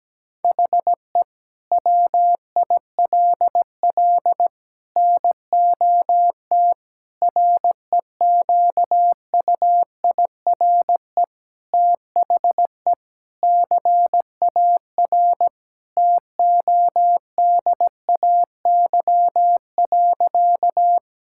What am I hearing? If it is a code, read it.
Morse code, heard as a keyed tone, about 17 wpm, 700 Hz. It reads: HE WILL NOT REQUIRE THE CAR TODAY.
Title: Sentences from Top 1000 Words - Review Part 6 - 17wpm